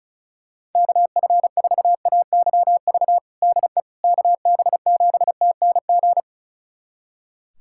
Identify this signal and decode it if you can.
Morse code, heard as a keyed tone, about 36 wpm, 700 Hz. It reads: KF4AYV DE KB7TNG
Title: USA Callsign DE Callsign - 6 Characters - 35wpm